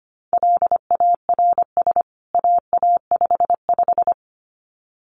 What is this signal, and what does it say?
Morse code, heard as a keyed tone, about 25 wpm, 700 Hz. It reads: LARS AA55
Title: CWT Exchange 25wpm